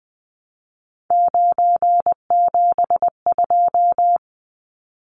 Morse code, heard as a keyed tone, about 20 wpm, 700 Hz. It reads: 972
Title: Numbers - 3-Digit 20wpm